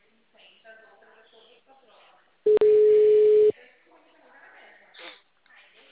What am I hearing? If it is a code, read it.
Ringback tone (Europe, 425 Hz), with speech.
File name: out- -2008- - - .32622